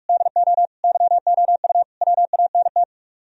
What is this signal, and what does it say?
Morse code, heard as a keyed tone, about 46 wpm, 700 Hz. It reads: DO YOU WANT